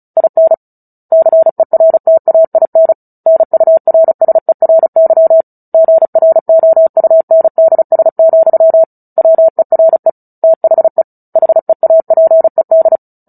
Morse code, heard as a keyed tone, about 34 wpm, 650 Hz. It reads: IN CERTAIN NURSERY GROUNDS, WERE THE HEAPED
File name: Book Two / 09-Chapter 9 - 35 WPM